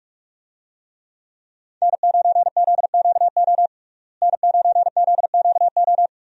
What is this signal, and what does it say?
Morse code, heard as a keyed tone, about 45 wpm, 700 Hz. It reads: N9ZQO N9ZQO
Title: USA Call Signs - 5 Characters - Repeated 2x - 45wpm